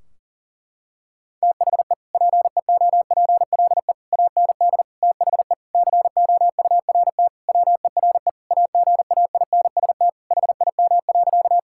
Morse code, heard as a keyed tone, about 37 wpm, 700 Hz. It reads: THE PEOPLE AND THE COURT WERE AGAINST HIM.
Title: Sentences from Top 1600 Words 40wpm